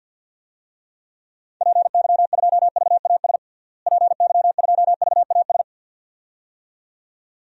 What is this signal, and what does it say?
Morse code, heard as a keyed tone, about 47 wpm, 700 Hz. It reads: PY2VAS PY2VAS
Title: DXCC-100 Call Signs - 6 Characters - Repeated 2x - 50wpm